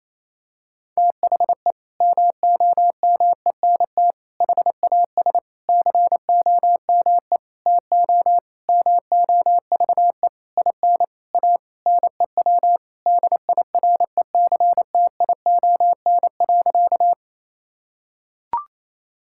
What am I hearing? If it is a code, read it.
Morse code, heard as a keyed tone, about 28 wpm, 700 Hz. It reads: THE MOMENT HAS COME TO MOVE IN A NEW DIRECTION.
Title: Sentences from Top 1000 Words - Review Part 5 - 28wpm